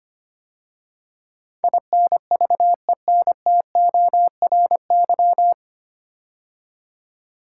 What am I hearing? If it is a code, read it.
Morse code, heard as a keyed tone, about 25 wpm, 690 Hz. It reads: INVENTORY
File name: practice / 25wpm / 9 Letter Words - First 3 Letters Given - 25wpm